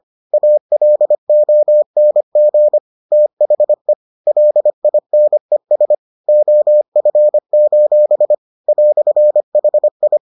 Morse code, heard as a keyed tone, about 25 wpm, 600 Hz. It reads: ALONG THE LINES OF: "HI